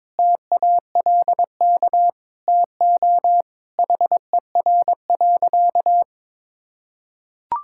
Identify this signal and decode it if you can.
Morse code, heard as a keyed tone, about 22 wpm, 700 Hz. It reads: TALK TO HER.
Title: Sentences from Top 1000 Words - Review Part 6 - 22wpm